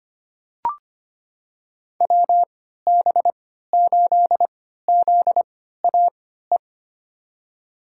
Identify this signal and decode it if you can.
Morse code, heard as a keyed tone, about 25 wpm, spaced as for 14 wpm, 700 Hz. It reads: WB8ZAE
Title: USA Call Signs - 6 Characters - 3x Character Spacing - 25wpm